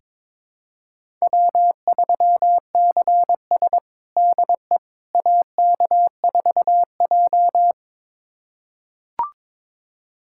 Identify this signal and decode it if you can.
Morse code, heard as a keyed tone, about 22 wpm, 700 Hz. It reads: W3CS DE AK4J
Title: USA Callsign DE Callsign - 4 Characters - 22wpm